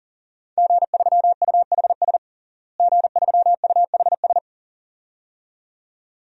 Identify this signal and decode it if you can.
Morse code, heard as a keyed tone, about 37 wpm, 700 Hz. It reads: G3UHS G3UHS
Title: DXCC-100 Call Signs - 5 Characters - Repeated 2x - 40wpm